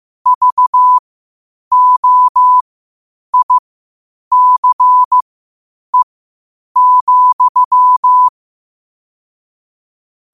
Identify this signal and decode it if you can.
Morse code, heard as a keyed tone, about 15 wpm, spaced as for 8 wpm, 1000 Hz. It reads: VOICE,
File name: qotdfiles / 8wpm